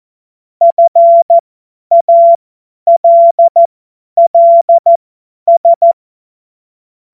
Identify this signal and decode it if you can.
Morse code, heard as a keyed tone, about 14 wpm, spaced as for 10 wpm, 680 Hz. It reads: FALLS
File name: Morsekurs / Witze / Witz 04. Januar 2022, 50 F70